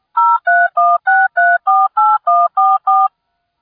DTMF, heard as telephone touch tones, tones of about 220 ms. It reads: 0316348144